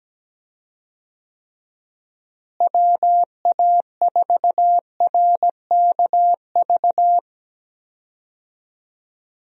Morse code, heard as a keyed tone, about 17 wpm, 700 Hz. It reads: WA4RKV
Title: USA Call Signs - 6 Characters - Repeated 3x - 17wpm